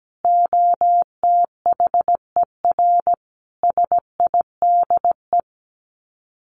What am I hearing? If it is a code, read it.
Morse code, heard as a keyed tone, about 17 wpm, 700 Hz. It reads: OTHER SIDE
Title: Sets of 4 Words 17wpm